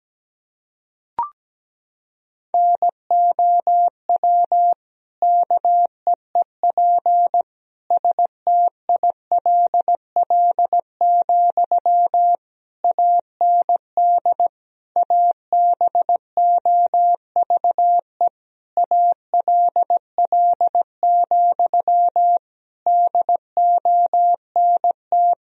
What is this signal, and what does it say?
Morse code, heard as a keyed tone, about 17 wpm, 700 Hz. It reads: NOW KEEP STILL, AND ABOVE ALL, DONT
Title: Sentences from Top 1000 Words - Review Part 2 - 17wpm